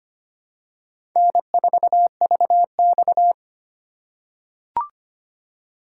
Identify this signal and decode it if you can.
Morse code, heard as a keyed tone, about 25 wpm, 700 Hz. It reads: N4VX
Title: USA Call Signs - 4 Characters - 3x Character Spacing - 25wpm